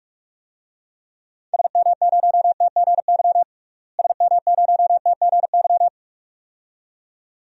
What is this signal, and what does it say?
Morse code, heard as a keyed tone, about 43 wpm, 700 Hz. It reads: SM0TGY SM0TGY
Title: DXCC-100 Call Signs - 6 Characters - Repeated 2x - 45wpm